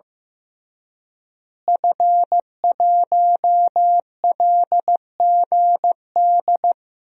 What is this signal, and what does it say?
Morse code, heard as a keyed tone, about 15 wpm, 700 Hz. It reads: F1LGD
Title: DXCC-100 Call Signs - 5 Characters - Repeated 3x - 15wpm